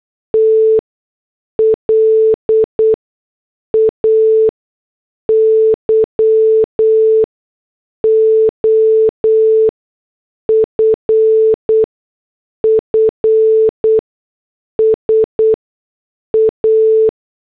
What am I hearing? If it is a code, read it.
Morse code, heard as a keyed tone, about 8 wpm, spaced as for 6 wpm, 440 Hz. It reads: TLAYOFFS A